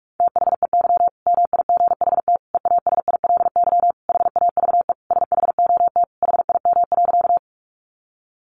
Morse code, heard as a keyed tone, about 44 wpm, 700 Hz. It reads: THEY MIGHT EASILY HAVE SHOT HIM.
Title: Sentences from Top 1100 Words 45wpm